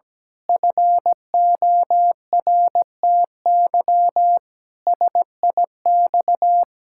Morse code, heard as a keyed tone, about 17 wpm, 700 Hz. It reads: FORTY SIX
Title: Pangram Challenge v2 - 17wpm